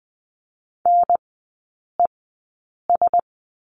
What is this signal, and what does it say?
Morse code, heard as a keyed tone, about 20 wpm, spaced as for 8 wpm, 700 Hz. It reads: NES
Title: 035 Lesson 7e - Letter s - Three Characters 20wpm